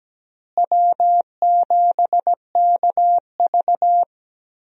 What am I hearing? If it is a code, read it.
Morse code, heard as a keyed tone, about 17 wpm, 700 Hz. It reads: W7KV
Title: USA Call Signs - 4 Characters - Repeated 2x - 17wpm